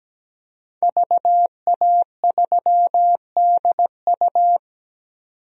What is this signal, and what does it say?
Morse code, heard as a keyed tone, about 17 wpm, 700 Hz. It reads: VA3DU
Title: DXCC-100 Call Signs - 5 Characters - 17wpm